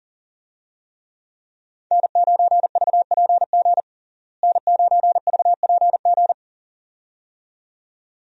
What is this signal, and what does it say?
Morse code, heard as a keyed tone, about 37 wpm, 700 Hz. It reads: N9VPG N9VPG
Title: USA Call Signs - 5 Characters - Repeated 2x - 40wpm